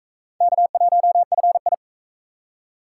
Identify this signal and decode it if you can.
Morse code, heard as a keyed tone, about 44 wpm, 700 Hz. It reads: K1FI